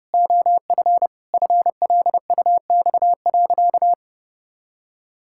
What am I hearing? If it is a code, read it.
Morse code, heard as a keyed tone, about 30 wpm, 700 Hz. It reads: OF FLUX.
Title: Pangram Challenge v2 - 30wpm